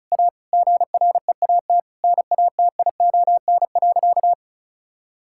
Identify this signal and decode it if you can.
Morse code, heard as a keyed tone, about 34 wpm, 700 Hz. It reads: A GREAT NATION.
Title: Sentences from Top 1000 Words - Review Part 6 - 35wpm